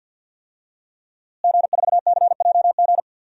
Morse code, heard as a keyed tone, about 49 wpm, 700 Hz. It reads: G4CJG
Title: DXCC-100 Call Signs - 5 Characters - Repeated 2x - 50wpm